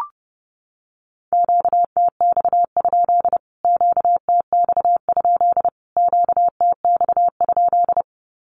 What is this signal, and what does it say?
Morse code, heard as a keyed tone, about 30 wpm, 700 Hz. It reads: QTX? QTX? QTX?
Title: Q-Codes Repeated 3x 30wpm